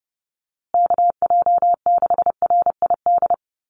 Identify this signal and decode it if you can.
Morse code, heard as a keyed tone, about 30 wpm, 700 Hz. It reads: KJ6RID
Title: USA Call Signs Repeated 3x 30wpm